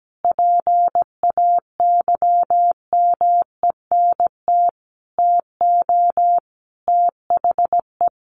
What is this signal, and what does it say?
Morse code, heard as a keyed tone, about 17 wpm, 700 Hz. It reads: PAYMENT TO THE
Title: Sentences from Top 1500 Words 17wpm